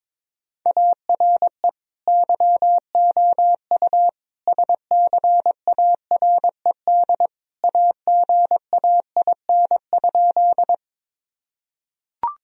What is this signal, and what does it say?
Morse code, heard as a keyed tone, about 21 wpm, 700 Hz. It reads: ARE YOU SCARED AGAIN?
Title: Sentences from Top 1000 Words - Review Part 4 - 22wpm